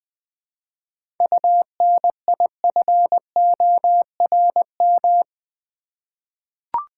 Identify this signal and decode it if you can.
Morse code, heard as a keyed tone, about 21 wpm, 700 Hz. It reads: UNIFORM
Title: 7 Letter Words - First 3 Letters Given - 20wpm